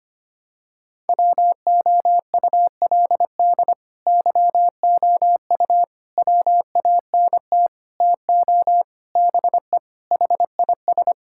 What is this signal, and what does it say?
Morse code, heard as a keyed tone, about 25 wpm, 700 Hz. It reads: WOULD YOU WANT TO BE HIS
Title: Sentences from Top 1600 Words 25wpm